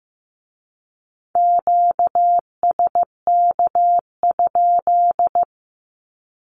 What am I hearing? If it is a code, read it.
Morse code, heard as a keyed tone, about 15 wpm, 700 Hz. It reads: QSK?
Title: Q-Codes with Spoken Meaning - 15wpm